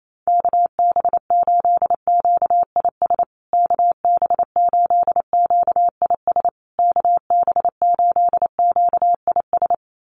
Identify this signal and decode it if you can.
Morse code, heard as a keyed tone, about 28 wpm, 700 Hz. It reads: KB8QIS KB8QIS KB8QIS
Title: USA Call Signs Repeated 3x 28wpm